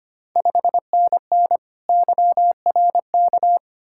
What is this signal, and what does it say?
Morse code, heard as a keyed tone, about 25 wpm, 690 Hz. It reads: 5NN YRK
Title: Ontario QSO Party Exchange 25wpm